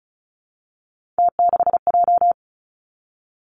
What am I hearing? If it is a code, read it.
Morse code, heard as a keyed tone, about 36 wpm, 700 Hz. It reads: T6J
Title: Three Challenging Letter-Number Combinations 35wpm